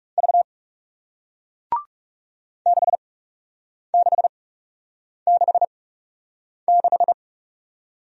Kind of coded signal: Morse code

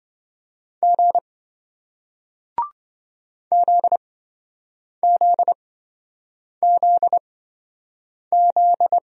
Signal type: Morse code

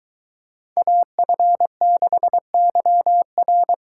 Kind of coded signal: Morse code

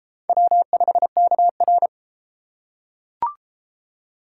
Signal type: Morse code

